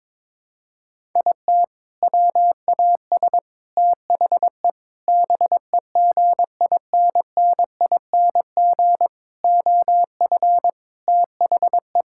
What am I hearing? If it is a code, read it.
Morse code, heard as a keyed tone, about 22 wpm, 700 Hz. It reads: IT WAS THE BEGINNING OF THE
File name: practice / 22wpm / Sentences with 4-Word Phrases - 22wpm